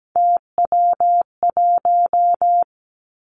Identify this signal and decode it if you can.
Morse code, heard as a keyed tone, about 17 wpm, 700 Hz. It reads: TW1
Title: Three Challenging Letter-Number Combinations 17wpm